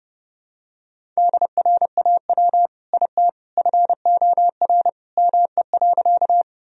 Morse code, heard as a keyed tone, about 32 wpm, 700 Hz. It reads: DRAW IT FOR ME.
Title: Sentences from Top 1000 Words - Review Part 1 - 30wpm